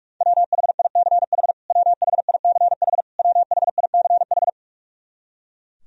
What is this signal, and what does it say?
Morse code, heard as a keyed tone, about 46 wpm, 700 Hz. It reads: WHICH WHICH WHICH
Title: Top 100 Words Repeated 45wpm